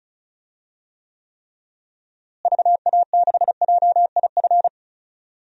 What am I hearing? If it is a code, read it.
Morse code, heard as a keyed tone, about 36 wpm, 700 Hz. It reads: VA6JIF
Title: DXCC-100 Call Signs - 6 Characters - Repeated 3x - 35wpm